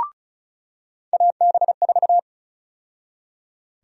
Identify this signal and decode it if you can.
Morse code, heard as a keyed tone, about 36 wpm, 700 Hz. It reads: AB4
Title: USA Call Sign Prefixes - 35wpm